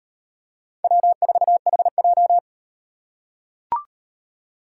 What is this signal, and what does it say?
Morse code, heard as a keyed tone, about 38 wpm, 700 Hz. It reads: W4HJ